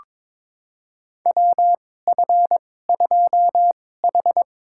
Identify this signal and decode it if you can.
Morse code, heard as a keyed tone, about 22 wpm, spaced as for 16 wpm, 700 Hz. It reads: WF2H
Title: USA Call Signs - 4 Characters - 2x Character Spacing - 22wpm